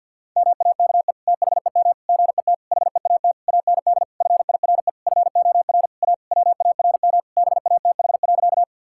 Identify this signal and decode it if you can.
Morse code, heard as a keyed tone, about 50 wpm, 700 Hz. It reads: MAKE THEM GET HEAT AND FIRE FOR A WARM BATH.
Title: Sentences from Top 1600 Words 50wpm